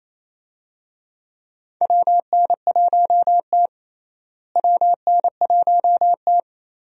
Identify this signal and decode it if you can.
Morse code, heard as a keyed tone, about 27 wpm, 700 Hz. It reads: WN1T WN1T
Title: USA Call Signs - 4 Characters - Repeated 3x - 28wpm